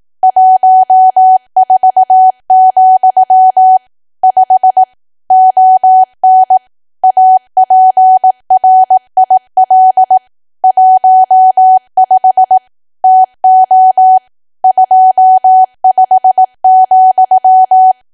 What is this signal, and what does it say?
Morse code, heard as a keyed tone, about 18 wpm, 750 Hz. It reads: 14, 5 ON APRIL 15 TO 25,
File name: cwa / W1AW-Level2 / 18WPM